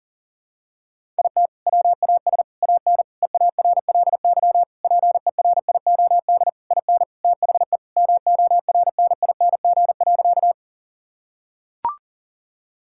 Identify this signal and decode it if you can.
Morse code, heard as a keyed tone, about 40 wpm, 700 Hz. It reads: IT WAS AN EARLY PERIOD IN THE MORNING.
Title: Sentences from Top 1000 Words - Review Part 5 - 40wpm